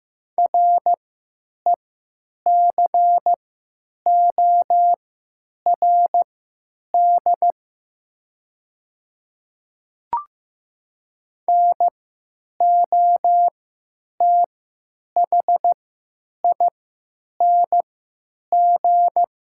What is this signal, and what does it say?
Morse code, heard as a keyed tone, about 15 wpm, spaced as for 9 wpm, 700 Hz. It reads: RECORD NOTHING
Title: 3x Character Spacing - Top 500 Words - 15wpm